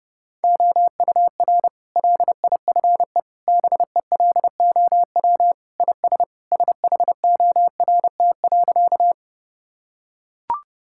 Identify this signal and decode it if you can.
Morse code, heard as a keyed tone, about 30 wpm, 700 Hz. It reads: OUR LIFE BELOW IS SHORT.
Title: Sentences from Top 1000 Words - Review Part 5 - 30wpm